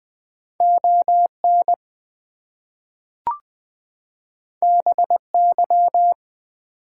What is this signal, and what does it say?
Morse code, heard as a keyed tone, about 20 wpm, 700 Hz. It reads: ON BY
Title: Two Letter Words - 20wpm